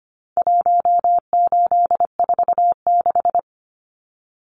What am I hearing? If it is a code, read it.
Morse code, heard as a keyed tone, about 25 wpm, 700 Hz. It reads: 1846
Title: Numbers - Contest Style - 4-Digit 25wpm